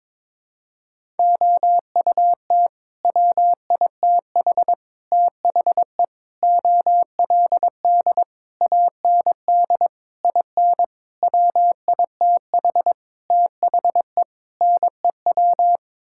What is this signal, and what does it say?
Morse code, heard as a keyed tone, about 22 wpm, 700 Hz. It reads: OUT WITH THE OLD AND IN WITH THE NEW
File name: Sets of 9 Words - 22wpm